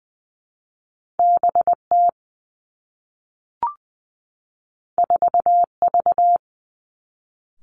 Morse code, heard as a keyed tone, about 20 wpm, 700 Hz. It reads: B T 4 V
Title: Two Challenging Letter-Number Combinations 20wpm